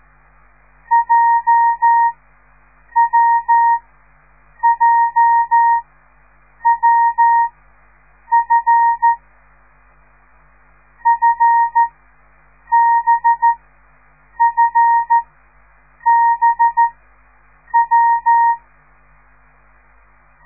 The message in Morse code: JWJWF FBFBW